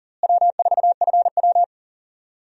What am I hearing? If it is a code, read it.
Morse code, heard as a keyed tone, about 39 wpm, 700 Hz. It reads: W4FW
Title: USA Call Signs - 4 Characters - Repeated 2x - 40wpm